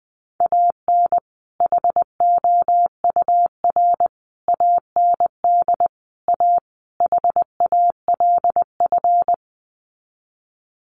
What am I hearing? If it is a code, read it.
Morse code, heard as a keyed tone, about 20 wpm, 700 Hz. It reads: AN HOUR AND A HALF
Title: Sets of 5 Words 20wpm